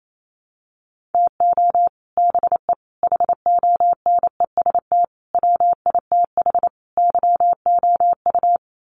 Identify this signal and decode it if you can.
Morse code, heard as a keyed tone, about 28 wpm, 700 Hz. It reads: TO BE HONEST WITH YOU
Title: Sets of 5 Words Encore 28wpm